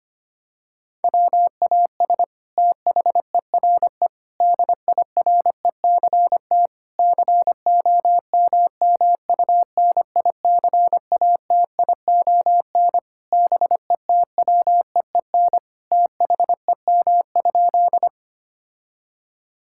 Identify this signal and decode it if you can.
Morse code, heard as a keyed tone, about 25 wpm, 700 Hz. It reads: WAS THERE DIRECT COMMUNICATION BETWEEN THEM?
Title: Sentences from Top 1000 Words - Review Part 6 - 25wpm